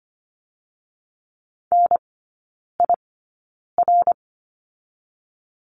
Morse code, heard as a keyed tone, about 25 wpm, spaced as for 9 wpm, 700 Hz. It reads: NIR
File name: Lesson 15e - Number 5 - Three Characters 25wpm